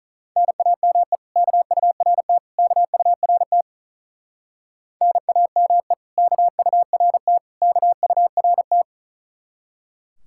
Morse code, heard as a keyed tone, about 45 wpm, 700 Hz. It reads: NAME KURT KURT NAME KURT KURT